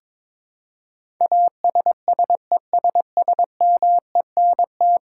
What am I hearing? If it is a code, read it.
Morse code, heard as a keyed tone, about 22 wpm, 700 Hz. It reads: ASSESSMENT
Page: morse-code-ninja-practice-22wpm